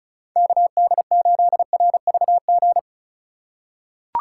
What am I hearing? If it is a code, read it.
Morse code, heard as a keyed tone, about 36 wpm, 700 Hz. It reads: KD8RVG
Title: USA Call Signs - 6 Characters - 2x Character Spacing - 35wpm